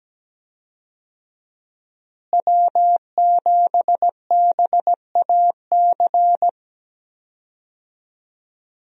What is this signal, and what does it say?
Morse code, heard as a keyed tone, about 17 wpm, 700 Hz. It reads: W7BAC
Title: USA Call Signs - 5 Characters - Repeated 3x - 17wpm